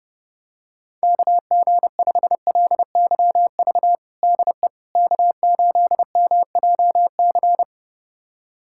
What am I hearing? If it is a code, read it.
Morse code, heard as a keyed tone, about 30 wpm, 700 Hz. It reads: KG5LYV DE K8MJC
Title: USA Callsign DE Callsign 30wpm